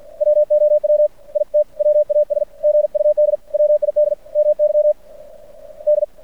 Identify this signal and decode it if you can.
Morse code, heard as a keyed tone, about 50 wpm, 600 Hz. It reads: WOW IT WAS GUD WID MY D
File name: archived / CRSnet